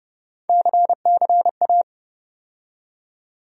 Morse code, heard as a keyed tone, about 32 wpm, 700 Hz. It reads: CCA
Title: USA Call Sign Postfixes - 3 Letters 30wpm